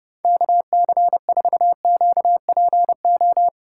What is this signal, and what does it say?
Morse code, heard as a keyed tone, about 32 wpm, 700 Hz. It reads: KC4QPO
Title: USA Callsign DE Callsign 30wpm